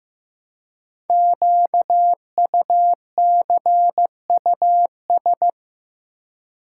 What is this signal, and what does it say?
Morse code, heard as a keyed tone, about 15 wpm, 700 Hz. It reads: QUCUS
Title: Code Groups - 5 Letters - 15wpm